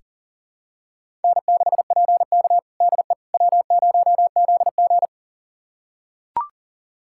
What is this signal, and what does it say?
Morse code, heard as a keyed tone, about 40 wpm, 700 Hz. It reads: N6PK DE W0ZG
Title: USA Callsign DE Callsign - 4 Characters - 40wpm